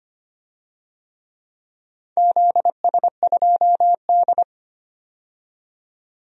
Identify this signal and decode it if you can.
Morse code, heard as a keyed tone, about 25 wpm, 700 Hz. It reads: ZS2D
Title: DXCC-100 Call Signs - 4 Characters - 25wpm